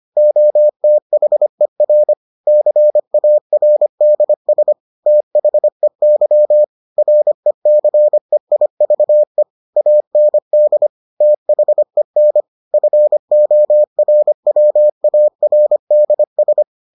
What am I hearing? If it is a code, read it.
Morse code, heard as a keyed tone, about 25 wpm, 600 Hz. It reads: OTHER CARDS THEY RECEIVE AND THEN FORWARDS